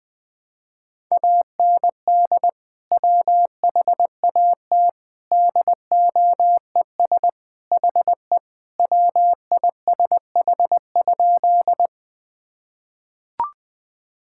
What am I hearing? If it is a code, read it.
Morse code, heard as a keyed tone, about 21 wpm, 700 Hz. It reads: AND WHAT DOES HE WISH?
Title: Sentences from Top 1600 Words 20wpm